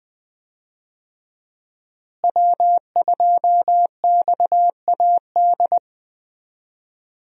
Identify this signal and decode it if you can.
Morse code, heard as a keyed tone, about 20 wpm, 700 Hz. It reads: W2XAD
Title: USA Call Signs - 5 Characters - Repeated 3x - 20wpm